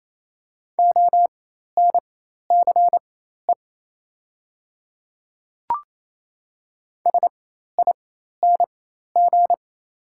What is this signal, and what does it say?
Morse code, heard as a keyed tone, about 28 wpm, spaced as for 13 wpm, 690 Hz. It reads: ONCE SING
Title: 4x Character Spacing - Top 500 Words - 28wpm